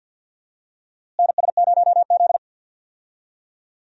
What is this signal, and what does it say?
Morse code, heard as a keyed tone, about 46 wpm, 700 Hz. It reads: NS0Z
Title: USA Call Signs - 4 Characters - Repeated 3x - 50wpm